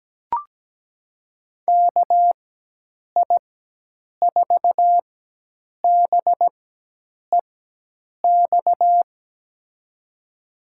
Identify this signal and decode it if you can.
Morse code, heard as a keyed tone, about 17 wpm, spaced as for 8 wpm, 700 Hz. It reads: KI4BEX
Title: USA Call Signs - 6 Characters - 4x Character Spacing - 17wpm